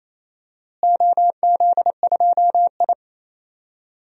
Morse code, heard as a keyed tone, about 28 wpm, 700 Hz. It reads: OZ2I
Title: DXCC-100 Call Signs - 4 Characters - Repeated 2x - 28wpm